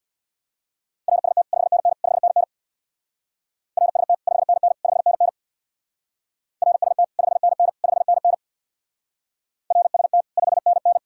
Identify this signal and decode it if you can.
Morse code, heard as a keyed tone, about 66 wpm, 700 Hz. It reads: RST 5NN 5NN RST 5NN 5NN RST 5NN 5NN RST 5NN